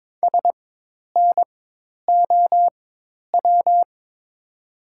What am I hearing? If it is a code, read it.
Morse code, heard as a keyed tone, about 22 wpm, spaced as for 10 wpm, 700 Hz. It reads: SNOW
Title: 4x Character Spacing - Top 500 Words - 22wpm